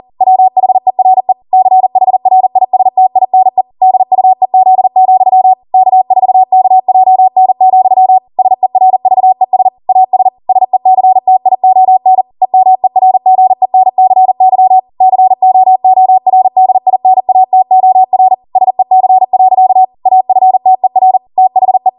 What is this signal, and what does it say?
Morse code, heard as a keyed tone, about 40 wpm, 750 Hz. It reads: WHERE CHRISTINE DUEZ, K4KJN, SERVES AS SECTION EMERGENCY COORDINATOR SEC. AFTER THE